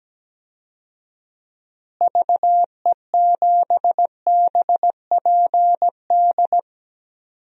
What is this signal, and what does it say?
Morse code, heard as a keyed tone, about 17 wpm, 700 Hz. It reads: VE7BPD
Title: DXCC-100 Call Signs - 6 Characters - Repeated 3x - 17wpm